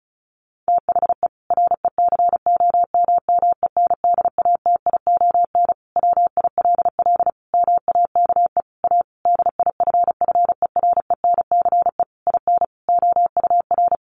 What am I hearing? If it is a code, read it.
Morse code, heard as a keyed tone, about 34 wpm, 700 Hz. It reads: THE RECOMMENDATION WILL MAKE A DIFFERENCE IN OUR
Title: Sentences from Top 1400 Words 35wpm